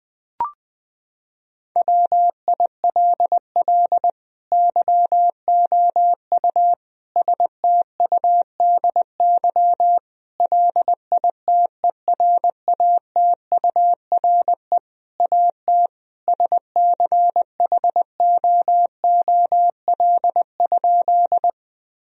Morse code, heard as a keyed tone, about 20 wpm, 700 Hz. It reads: WILL YOU STUDY LITERATURE AT SCHOOL?
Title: Sentences from Top 1000 Words - Review Part 3 - 20wpm